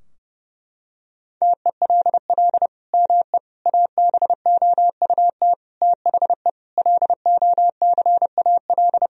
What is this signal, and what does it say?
Morse code, heard as a keyed tone, about 30 wpm, 700 Hz. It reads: TELL ME ABOUT THE LOCAL
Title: Sentences from Top 1000 Words - Review Part 3 - 30wpm